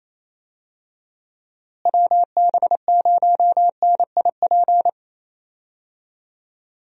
Morse code, heard as a keyed tone, about 28 wpm, 700 Hz. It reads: WB0NIP
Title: USA Call Signs - 6 Characters - Repeated 2x - 28wpm